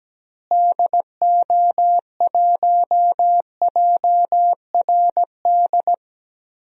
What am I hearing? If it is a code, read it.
Morse code, heard as a keyed tone, about 17 wpm, 700 Hz. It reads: DO1JRD
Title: DXCC-100 Call Signs - 6 Characters - Repeated 3x - 17wpm